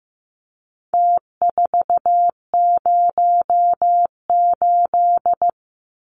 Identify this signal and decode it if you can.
Morse code, heard as a keyed tone, about 15 wpm, 700 Hz. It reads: T408
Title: Numbers - Contest Style - 4-Digit 15wpm